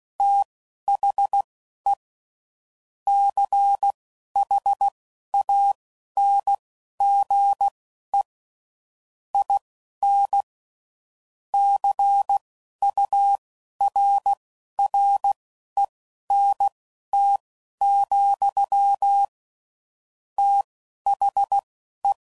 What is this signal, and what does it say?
Morse code, heard as a keyed tone, about 16 wpm, spaced as for 11 wpm, 800 Hz. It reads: THE CHANGE IN CURRENT, THE
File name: CW Code Groups @10WPM / Technical Subjects / XL